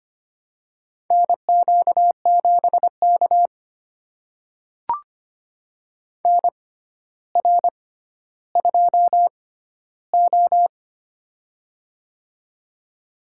Morse code, heard as a keyed tone, about 25 wpm, spaced as for 13 wpm, 700 Hz. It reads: NQ7K NR2O
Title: USA Call Signs - 4 Characters - 6x Character Spacing - 25wpm